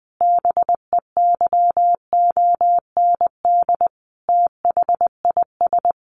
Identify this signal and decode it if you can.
Morse code, heard as a keyed tone, about 19 wpm, 700 Hz. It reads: BEYOND THIS
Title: Sentences from Top 1500 Words 20wpm